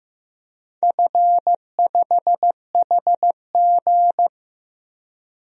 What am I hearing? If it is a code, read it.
Morse code, heard as a keyed tone, about 15 wpm, 700 Hz. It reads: F5HG
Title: DXCC-100 Call Signs - 4 Characters - Repeated 3x - 15wpm